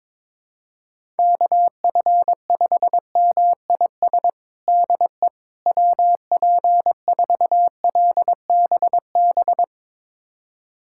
Morse code, heard as a keyed tone, about 22 wpm, 700 Hz. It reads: KF5MIS DE WP4LBB
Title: USA Callsign DE Callsign - 6 Characters - 22wpm